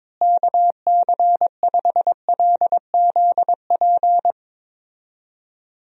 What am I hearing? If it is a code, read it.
Morse code, heard as a keyed tone, about 22 wpm, 700 Hz. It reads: KC5LZP
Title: USA Call Signs - 6 Characters - Repeated 2x - 22wpm